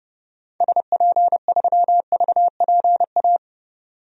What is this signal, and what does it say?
Morse code, heard as a keyed tone, about 32 wpm, 700 Hz. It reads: SP3VPA
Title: DXCC-100 Call Signs - 6 Characters - 30wpm